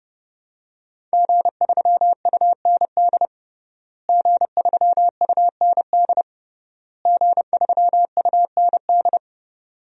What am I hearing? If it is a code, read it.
Morse code, heard as a keyed tone, about 30 wpm, 700 Hz. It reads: G3UND G3UND G3UND
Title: DXCC-100 Call Signs - 5 Characters - Repeated 3x - 30wpm